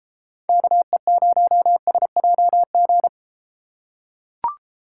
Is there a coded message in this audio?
Yes: Morse code from 0.5 to 3.1 s, heard as a keyed tone, about 31 wpm, 700 Hz. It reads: KE0SJG